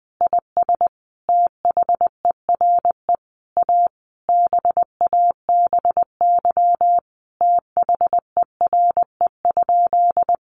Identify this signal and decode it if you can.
Morse code, heard as a keyed tone, about 20 wpm, 700 Hz. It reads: IS THERE A BABY THERE?
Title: Sentences from Top 1300 Words 20wpm